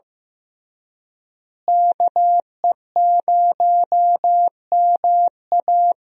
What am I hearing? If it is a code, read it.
Morse code, heard as a keyed tone, about 15 wpm, 700 Hz. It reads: KE0MA
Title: USA Call Signs - 5 Characters - Repeated 3x - 15wpm